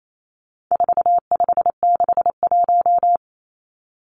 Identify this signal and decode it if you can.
Morse code, heard as a keyed tone, about 28 wpm, 700 Hz. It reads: 4561
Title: Numbers - 4-Digit 28wpm